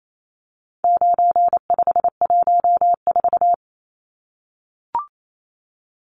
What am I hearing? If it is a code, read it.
Morse code, heard as a keyed tone, about 28 wpm, 700 Hz. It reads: 9514
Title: Numbers - 4-Digit 28wpm